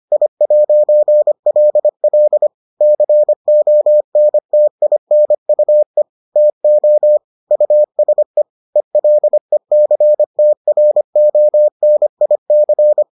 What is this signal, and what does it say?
Morse code, heard as a keyed tone, about 25 wpm, 600 Hz. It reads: I'LL CONTINUE TO USE ELECTRONIC